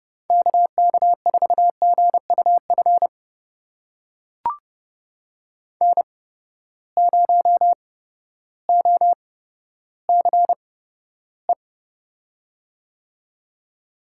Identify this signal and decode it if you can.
Morse code, heard as a keyed tone, about 30 wpm, spaced as for 14 wpm, 700 Hz. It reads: KK4GUF N0OCE
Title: USA Call Signs - 8x Character Spacing - 30wpm